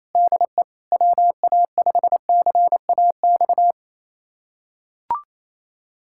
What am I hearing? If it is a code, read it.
Morse code, heard as a keyed tone, about 28 wpm, 700 Hz. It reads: DE WA5CAX
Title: USA Callsign DE Callsign - 6 Characters - 28wpm